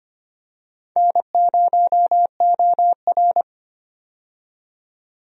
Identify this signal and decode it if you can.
Morse code, heard as a keyed tone, about 25 wpm, 700 Hz. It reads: N0OR
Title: USA Call Signs - 4 Characters - 25wpm